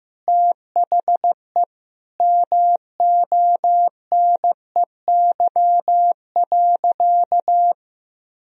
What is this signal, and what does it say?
Morse code, heard as a keyed tone, about 15 wpm, 700 Hz. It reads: THE MONEY.
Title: Sentences from Top 1000 Words - Review Part 2 - 15wpm